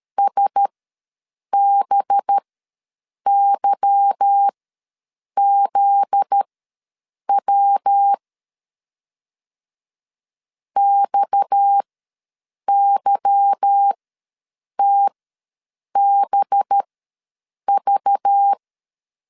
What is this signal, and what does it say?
Morse code, heard as a keyed tone, about 13 wpm, spaced as for 7 wpm, 800 Hz. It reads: SBYZW XYTBV
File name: LEZIONI / 8 YJ45 / 8 YJ4 ws 12wpm 3249